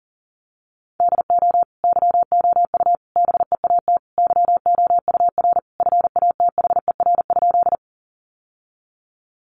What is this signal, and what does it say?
Morse code, heard as a keyed tone, about 37 wpm, 700 Hz. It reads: DO YOU BEAT YOUR FATHER?
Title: Sentences from Top 1300 Words 40wpm